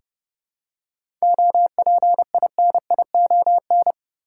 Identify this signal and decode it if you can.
Morse code, heard as a keyed tone, about 30 wpm, 700 Hz. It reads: OPINION
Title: Seven Letter Words - 30wpm